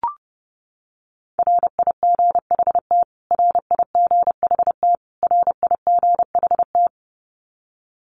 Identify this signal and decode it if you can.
Morse code, heard as a keyed tone, about 30 wpm, 700 Hz. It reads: RIGHT RIGHT RIGHT
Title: Top 500 Words Repeated 30wpm